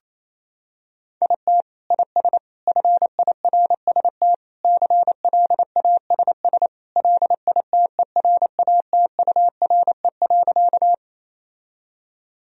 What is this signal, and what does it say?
Morse code, heard as a keyed tone, about 27 wpm, 700 Hz. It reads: IT IS FIRST CLASS LITERATURE.
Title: Sentences from Top 1000 Words - Review Part 3 - 28wpm